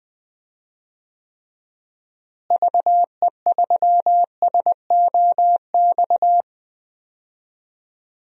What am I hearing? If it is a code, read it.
Morse code, heard as a keyed tone, about 19 wpm, 700 Hz. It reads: VE3SOX
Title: DXCC-100 Call Signs - 6 Characters - Repeated 3x - 20wpm